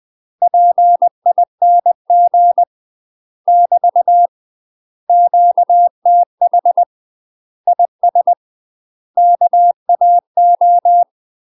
Morse code, heard as a keyed tone, about 20 wpm, 700 Hz. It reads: PING <BT> QTH IS KAO
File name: UK MASTER QSOs TRAINING / 20WPM / UK MASTER QSOs 0049